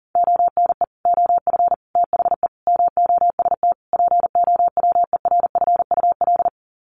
Morse code, heard as a keyed tone, about 40 wpm, 700 Hz. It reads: ONE OF THE MOST POWERFUL
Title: Sets of 5 Words Encore 40wpm